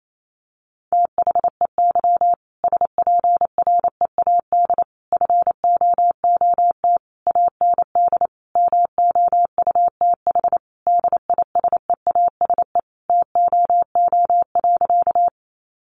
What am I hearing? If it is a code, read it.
Morse code, heard as a keyed tone, about 28 wpm, 700 Hz. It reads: THEY SPREAD FOOT AND MOUTH DISEASE TOO.
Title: Sentences from Top 1500 Words 28wpm